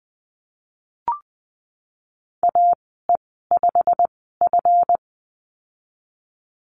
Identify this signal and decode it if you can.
Morse code, heard as a keyed tone, about 20 wpm, spaced as for 15 wpm, 700 Hz. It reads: AE5F
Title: 167 Lesson 27g - Number 7 - Call Signs 20wpm